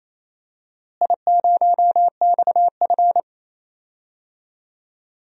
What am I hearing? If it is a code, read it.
Morse code, heard as a keyed tone, about 28 wpm, 700 Hz. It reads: I0XF
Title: DXCC-100 Call Signs - 4 Characters - Repeated 2x - 28wpm